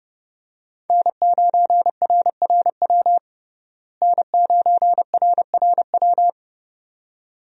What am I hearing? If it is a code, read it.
Morse code, heard as a keyed tone, about 30 wpm, 700 Hz. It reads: N9RRW N9RRW
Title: USA Call Signs - 5 Characters - Repeated 2x - 30wpm